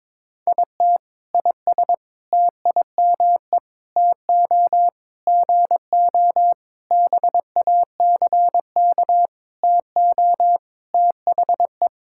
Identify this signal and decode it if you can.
Morse code, heard as a keyed tone, about 22 wpm, 700 Hz. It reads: IT IS TIME TO GO BACK TO THE
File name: Sentences with 7-Word Phrases - 22wpm